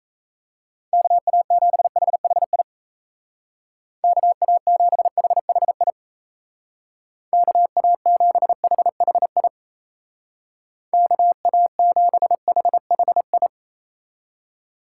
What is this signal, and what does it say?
Morse code, heard as a keyed tone, about 34 wpm, 700 Hz. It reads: KA7HHI KA7HHI KA7HHI KA7HHI
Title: Speed Racing - USA Call Signs 28wpm